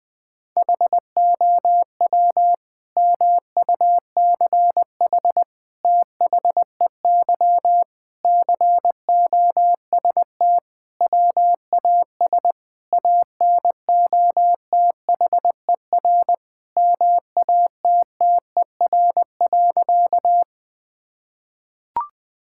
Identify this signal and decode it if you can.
Morse code, heard as a keyed tone, about 20 wpm, 700 Hz. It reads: HOW MUCH THEY COST WAS ANOTHER MATTER.
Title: Sentences from Top 1000 Words - Review Part 5 - 20wpm